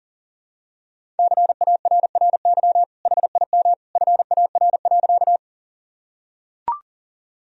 Morse code, heard as a keyed tone, about 40 wpm, 700 Hz. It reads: CARRY HIM FAR.
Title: Sentences from Top 1000 Words - Review Part 1 - 40wpm